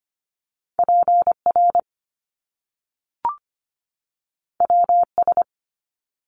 Morse code, heard as a keyed tone, about 25 wpm, 700 Hz. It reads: PR WS
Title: Top Two Letter Combinations in Words 25wpm